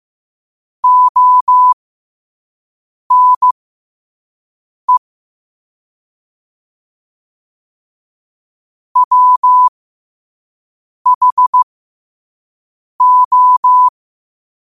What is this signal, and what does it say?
Morse code, heard as a keyed tone, about 15 wpm, spaced as for 5 wpm, 1000 Hz. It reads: ONE WHO
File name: qotdfiles / 5wpm